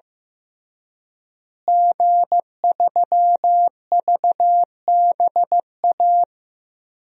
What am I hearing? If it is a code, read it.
Morse code, heard as a keyed tone, about 15 wpm, 700 Hz. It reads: G3VBA
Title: DXCC-100 Call Signs - 5 Characters - Repeated 3x - 15wpm